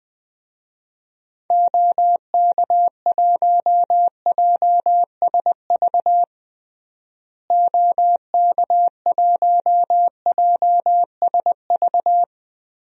Morse code, heard as a keyed tone, about 20 wpm, 700 Hz. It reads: OK1JSV OK1JSV
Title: DXCC-100 Call Signs - 6 Characters - Repeated 2x - 20wpm